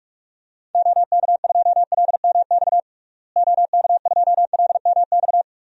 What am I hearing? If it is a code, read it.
Morse code, heard as a keyed tone, about 46 wpm, 700 Hz. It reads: OK2LMX OK2LMX
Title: DXCC-100 Call Signs - 6 Characters - Repeated 3x - 45wpm